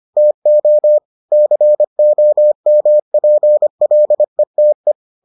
Morse code, heard as a keyed tone, about 25 wpm, 600 Hz. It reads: TO COMPLETE